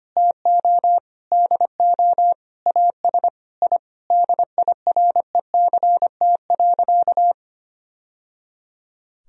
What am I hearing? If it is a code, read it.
Morse code, heard as a keyed tone, about 25 wpm, 700 Hz. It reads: TO DO AS I DIRECT.
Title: Sentences from Top 1000 Words - Review Part 2 - 25wpm